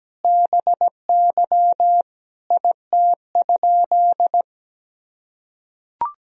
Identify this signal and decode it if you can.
Morse code, heard as a keyed tone, about 17 wpm, 700 Hz. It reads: BY IT?
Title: Sentences from Top 100 Words and Abbreviations 17wpm